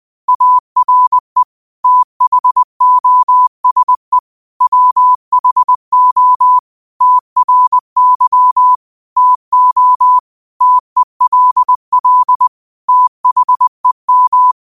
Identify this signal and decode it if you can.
Morse code, heard as a keyed tone, about 20 wpm, 1000 Hz. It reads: ARE THOSE WHO TRY TO TELL THEM
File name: 20wpm